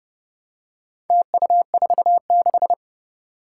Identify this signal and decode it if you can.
Morse code, heard as a keyed tone, about 30 wpm, 700 Hz. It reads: TU46
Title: Four Challenging Letter-Number Combinations 30wpm